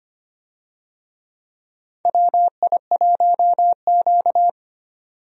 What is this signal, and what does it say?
Morse code, heard as a keyed tone, about 25 wpm, 700 Hz. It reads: WI1Q
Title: USA Call Signs - 4 Characters - 25wpm